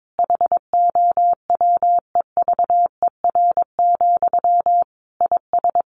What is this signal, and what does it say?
Morse code, heard as a keyed tone, about 22 wpm, 700 Hz. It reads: HOWEVER, IS
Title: Sentences from Top 1500 Words 22wpm